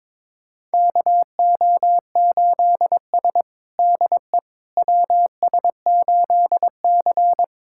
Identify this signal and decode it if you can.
Morse code, heard as a keyed tone, about 22 wpm, 700 Hz. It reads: KO8S DE WS8C
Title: USA Callsign DE Callsign - 4 Characters - 22wpm